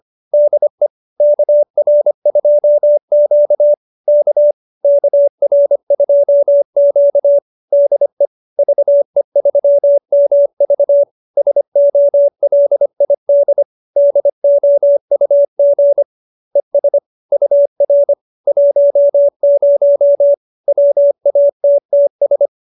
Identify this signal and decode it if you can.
Morse code, heard as a keyed tone, about 25 wpm, 570 Hz. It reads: DE KR2Q K KR2Q DE VE3MV SOLID DOUG ES UR 10 WATTS